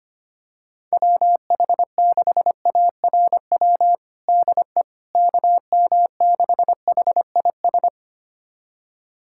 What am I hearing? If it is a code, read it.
Morse code, heard as a keyed tone, about 25 wpm, 700 Hz. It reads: WH6ARW DE KM6HIS
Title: USA Callsign DE Callsign - 6 Characters - 25wpm